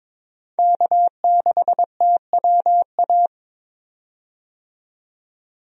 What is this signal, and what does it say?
Morse code, heard as a keyed tone, about 22 wpm, 700 Hz. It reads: K6TWA
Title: USA Call Signs - 5 Characters - 22wpm